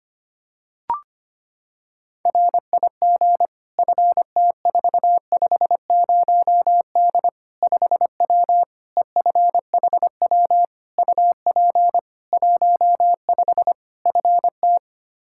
Morse code, heard as a keyed tone, about 25 wpm, 700 Hz. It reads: RIG FT450D 5W EFHW UP 15 FT
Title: Practice QSO - Exchange 2 - Info - 25wpm